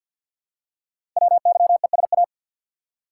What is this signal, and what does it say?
Morse code, heard as a keyed tone, about 48 wpm, 700 Hz. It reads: WYESA
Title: Code Groups - 5 Letters - 50wpm